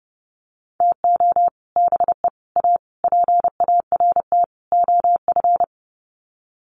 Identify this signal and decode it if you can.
Morse code, heard as a keyed tone, about 30 wpm, 700 Hz. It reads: TO BE A PART OF
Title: Sets of 5 Words 30wpm